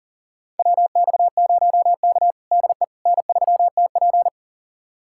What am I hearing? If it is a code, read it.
Morse code, heard as a keyed tone, about 37 wpm, 700 Hz. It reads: WX0K DE N3TP
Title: USA Callsign DE Callsign - 4 Characters - 40wpm